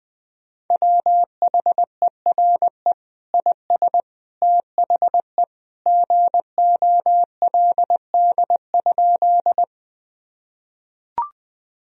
Morse code, heard as a keyed tone, about 21 wpm, 700 Hz. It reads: WHERE IS THE GOLD?
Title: Sentences from Top 1000 Words - Review Part 3 - 20wpm